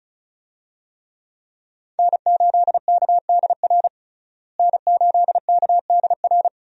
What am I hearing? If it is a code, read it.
Morse code, heard as a keyed tone, about 36 wpm, 700 Hz. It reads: N8KDR N8KDR
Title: USA Call Signs - 5 Characters - Repeated 3x - 35wpm